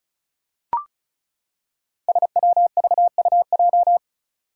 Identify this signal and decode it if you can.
Morse code, heard as a keyed tone, about 36 wpm, 700 Hz. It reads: SWVUJ
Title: Five Challenging Letter-Number Combinations 35wpm